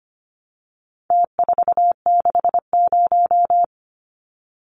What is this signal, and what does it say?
Morse code, heard as a keyed tone, about 25 wpm, 700 Hz. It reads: T460
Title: Numbers - Contest Style - 4-Digit 25wpm